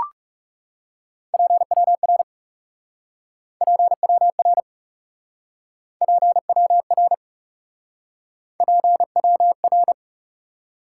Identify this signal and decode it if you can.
Morse code, heard as a keyed tone, about 39 wpm, 700 Hz. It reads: PWR PWR PWR PWR